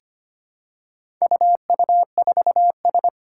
Morse code, heard as a keyed tone, about 25 wpm, 700 Hz. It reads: UU4S
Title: Four Challenging Letter-Number Combinations 25wpm